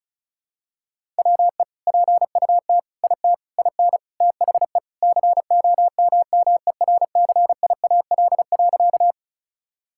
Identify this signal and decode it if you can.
Morse code, heard as a keyed tone, about 36 wpm, 700 Hz. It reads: WE PUT IT IN THE COMMERCIAL.
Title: Sentences from Top 1000 Words - Review Part 5 - 35wpm